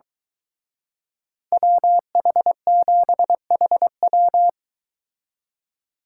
Morse code, heard as a keyed tone, about 24 wpm, 700 Hz. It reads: WH7HW